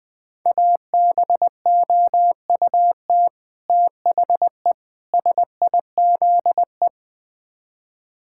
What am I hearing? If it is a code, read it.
Morse code, heard as a keyed tone, about 20 wpm, 700 Hz. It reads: ABOUT THE SIZE
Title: Build A Phrase - 5-Word Phrases - 20wpm